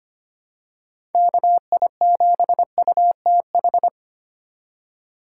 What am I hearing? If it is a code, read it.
Morse code, heard as a keyed tone, about 25 wpm, 700 Hz. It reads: KI7UTH
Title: USA Call Signs - 2x Character Spacing - 25wpm